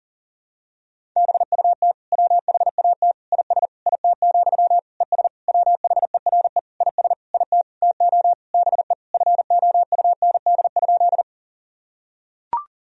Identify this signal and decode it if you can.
Morse code, heard as a keyed tone, about 40 wpm, 700 Hz. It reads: BUT WHAT IS IT, ES WHERE IS IT TO BE FOUND?
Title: Sentences from Top 300 Words and Abbreviations 40wpm